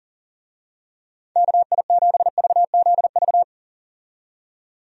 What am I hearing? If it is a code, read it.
Morse code, heard as a keyed tone, about 40 wpm, 700 Hz. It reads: KI7VZV
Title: USA Call Signs - 6 Characters - 40wpm